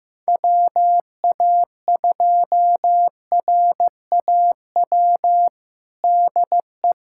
Morse code, heard as a keyed tone, about 16 wpm, 700 Hz. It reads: WA2RAW DE
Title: USA Callsign DE Callsign - 6 Characters - 15wpm